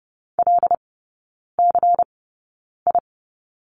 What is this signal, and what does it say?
Morse code, heard as a keyed tone, about 30 wpm, spaced as for 9 wpm, 700 Hz. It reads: LCI
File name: Lesson 16e - Letter c - Three Characters 30wpm